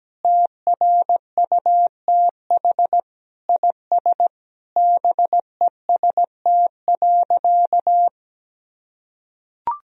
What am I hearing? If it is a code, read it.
Morse code, heard as a keyed tone, about 17 wpm, 700 Hz. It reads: TRUTH IS BEST.
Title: Sentences from Top 1000 Words - Review Part 2 - 17wpm